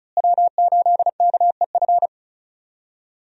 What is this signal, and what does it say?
Morse code, heard as a keyed tone, about 36 wpm, 700 Hz. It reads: W8KEF
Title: USA Call Signs - 5 Characters - Repeated 2x - 35wpm